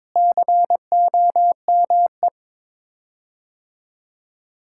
Morse code, heard as a keyed tone, about 22 wpm, 700 Hz. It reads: COME
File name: practice / 22wpm / Sets of 7 Words - 22wpm